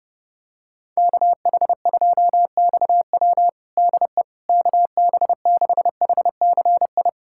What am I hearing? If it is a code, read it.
Morse code, heard as a keyed tone, about 30 wpm, 700 Hz. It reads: KH2XW DE KB6HCI
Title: USA Callsign DE Callsign 30wpm